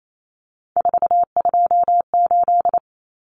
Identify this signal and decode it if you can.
Morse code, heard as a keyed tone, about 28 wpm, 700 Hz. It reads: 428
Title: Common Cut Numbers - 3-Digit 28wpm